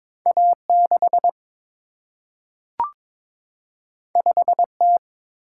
Morse code, heard as a keyed tone, about 22 wpm, 700 Hz. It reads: A6 5T
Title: Single-Letter Single-Number 22wpm